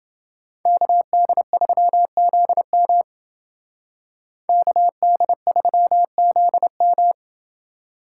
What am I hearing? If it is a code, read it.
Morse code, heard as a keyed tone, about 29 wpm, 700 Hz. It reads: KD3ZM KD3ZM